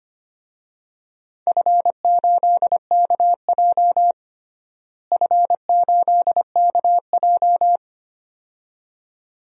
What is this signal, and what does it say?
Morse code, heard as a keyed tone, about 25 wpm, 700 Hz. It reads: F8KJ F8KJ
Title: DXCC-100 Call Signs - 4 Characters - Repeated 2x - 25wpm